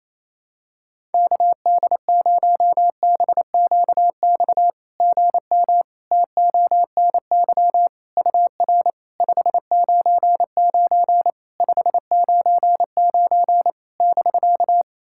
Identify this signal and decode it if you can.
Morse code, heard as a keyed tone, about 27 wpm, 690 Hz. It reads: KD0BQX GM TONY UR 599 599 <BK>
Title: SOTA QSO Practice 28wpm